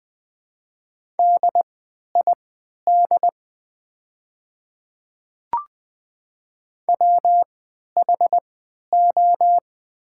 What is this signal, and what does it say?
Morse code, heard as a keyed tone, about 20 wpm, spaced as for 11 wpm, 700 Hz. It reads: DID WHO
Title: 3x Character Spacing - Top 100 Words - 20wpm